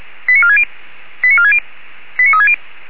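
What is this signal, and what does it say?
Five-tone selective call (ZVEI), heard as quick runs of five tones, tones of about 70 ms. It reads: 89460 89460 89360
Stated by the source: ZVEI five-tone selective call received on an SDR, from the Feuerwehr Orlamünde fire brigade audio archive.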